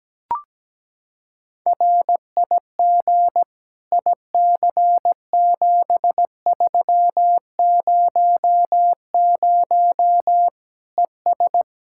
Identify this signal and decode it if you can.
Morse code, heard as a keyed tone, about 17 wpm, 700 Hz. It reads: RIG IC7300 ES
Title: Practice QSO - Exchange 2 - Info - 17wpm